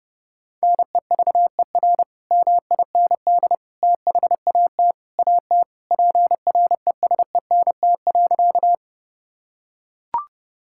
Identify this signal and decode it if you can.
Morse code, heard as a keyed tone, about 30 wpm, 700 Hz. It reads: NEVER MIND THAT AT PRESENT.
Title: Sentences from Top 1000 Words - Review Part 5 - 30wpm